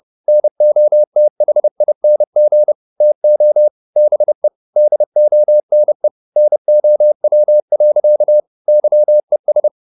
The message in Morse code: NOTHING TO BE DONE NOW. YES

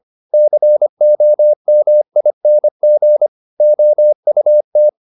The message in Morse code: COMING OUT

